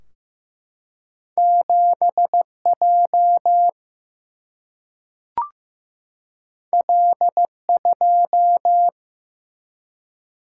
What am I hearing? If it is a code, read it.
Morse code, heard as a keyed tone, about 15 wpm, 700 Hz. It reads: 7J L2
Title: Single-Letter Single-Number 15wpm